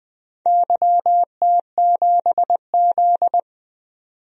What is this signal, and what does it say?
Morse code, heard as a keyed tone, about 20 wpm, 700 Hz. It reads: YT7Z
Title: DXCC-100 Call Signs - 4 Characters - 20wpm